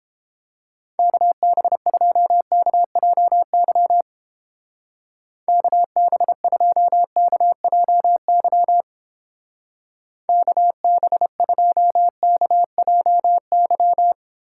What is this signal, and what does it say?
Morse code, heard as a keyed tone, about 29 wpm, 700 Hz. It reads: KB2KJY KB2KJY KB2KJY